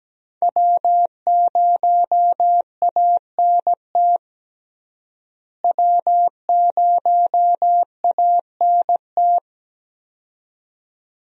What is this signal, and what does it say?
Morse code, heard as a keyed tone, about 17 wpm, 700 Hz. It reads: W0ANT W0ANT
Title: USA Call Signs - 5 Characters - Repeated 2x - 17wpm